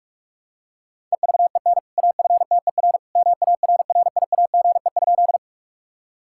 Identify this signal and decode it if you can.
Morse code, heard as a keyed tone, about 46 wpm, 700 Hz. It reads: EVEN AFTER MARRIAGE?
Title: Sentences from Top 1000 Words - Review Part 5 - 45wpm